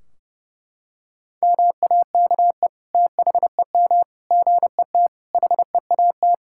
Morse code, heard as a keyed tone, about 30 wpm, 700 Hz. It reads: MAKE THEM GET HEAT
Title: Sentences from Top 1600 Words 30wpm